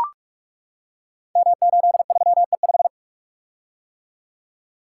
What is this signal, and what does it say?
Morse code, heard as a keyed tone, about 44 wpm, 700 Hz. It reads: M83E5
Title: Code Groups - 5 Letter:Numbers - 45wpm